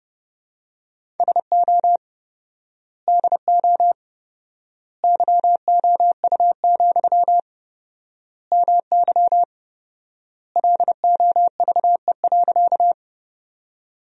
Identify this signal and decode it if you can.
Morse code, heard as a keyed tone, about 30 wpm, 700 Hz. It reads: SO DO YOU, MY LOVE.
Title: 4x Word-Spacing - Sentences from Top 400 Words - 30wpm